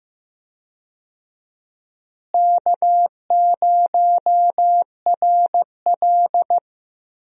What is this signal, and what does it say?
Morse code, heard as a keyed tone, about 15 wpm, 700 Hz. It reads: K0RL
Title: USA Call Signs - 4 Characters - Repeated 2x - 15wpm